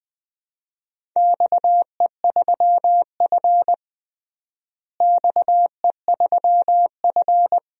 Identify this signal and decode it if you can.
Morse code, heard as a keyed tone, about 20 wpm, 700 Hz. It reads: XE3F XE3F
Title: DXCC-100 Call Signs - 4 Characters - Repeated 2x - 20wpm